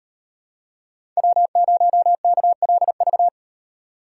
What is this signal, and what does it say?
Morse code, heard as a keyed tone, about 38 wpm, 700 Hz. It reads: W0KLV